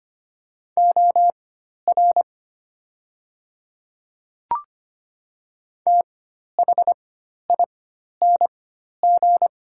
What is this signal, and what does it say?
Morse code, heard as a keyed tone, about 25 wpm, spaced as for 12 wpm, 700 Hz. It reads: OR THING
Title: 4x Character Spacing - Top 100 Words - 25wpm